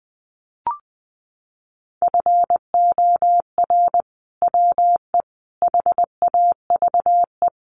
Morse code, heard as a keyed tone, about 20 wpm, 700 Hz. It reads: FOR WE HAVE
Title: Sentences from Top 1000 Words 20wpm